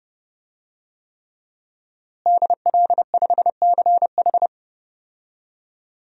Morse code, heard as a keyed tone, about 31 wpm, 700 Hz. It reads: DL5CH
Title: DXCC-100 Call Signs - 5 Characters - Repeated 3x - 30wpm